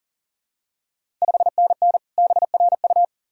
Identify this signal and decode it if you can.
Morse code, heard as a keyed tone, about 40 wpm, 700 Hz. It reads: 5NN BRU
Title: Ontario QSO Party Exchange 40wpm